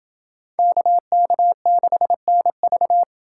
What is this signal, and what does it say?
Morse code, heard as a keyed tone, about 29 wpm, 700 Hz. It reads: KK6NV